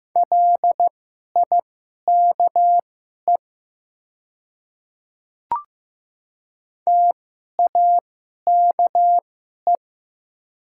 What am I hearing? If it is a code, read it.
Morse code, heard as a keyed tone, about 15 wpm, spaced as for 11 wpm, 700 Hz. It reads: LIKE TAKE
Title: 2x Character Spacing - Top 200 Words - 15wpm